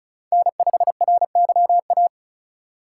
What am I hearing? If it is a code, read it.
Morse code, heard as a keyed tone, about 36 wpm, 700 Hz. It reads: N5RYA
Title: USA Call Signs - 5 Characters - 2x Character Spacing - 35wpm